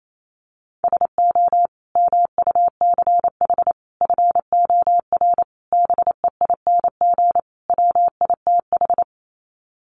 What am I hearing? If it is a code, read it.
Morse code, heard as a keyed tone, about 28 wpm, 690 Hz. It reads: SO MUCH FOR BEING WITH
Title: Sets of 5 Words Encore 28wpm